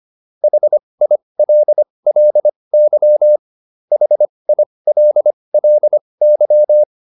Morse code, heard as a keyed tone, about 25 wpm, spaced as for 20 wpm, 600 Hz. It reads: HILLY HILLY